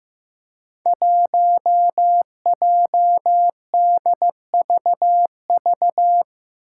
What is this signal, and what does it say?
Morse code, heard as a keyed tone, about 15 wpm, 700 Hz. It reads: 1JDVV
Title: Five Challenging Letter-Number Combinations 15wpm